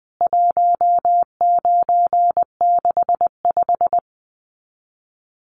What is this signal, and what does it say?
Morse code, heard as a keyed tone, about 20 wpm, 700 Hz. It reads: 1965
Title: CWT Exchange 20wpm